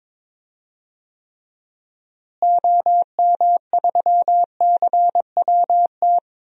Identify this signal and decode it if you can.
Morse code, heard as a keyed tone, about 22 wpm, 700 Hz. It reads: OM3CWT
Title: DXCC-100 Call Signs - 6 Characters - Repeated 2x - 22wpm